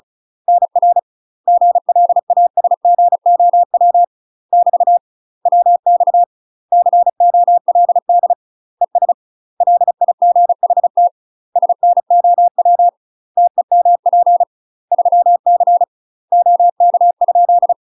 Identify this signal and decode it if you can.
Morse code, heard as a keyed tone, about 36 wpm, 700 Hz. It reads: NR GLASGOW <BT> WX COLD ES LIGHT SNOW TEMP 3C OK?
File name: UK MASTER QSOs TRAINING / 35WPM / UK MASTER QSOs 0010